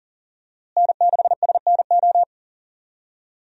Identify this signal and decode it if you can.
Morse code, heard as a keyed tone, about 40 wpm, 700 Hz. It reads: N6SNO
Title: USA Call Signs - 5 Characters - Repeated 2x - 40wpm